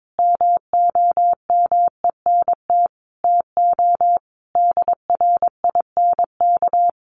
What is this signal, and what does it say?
Morse code, heard as a keyed tone, about 22 wpm, 700 Hz. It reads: MOMENT TO DRINK
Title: Sentences from Top 1500 Words 22wpm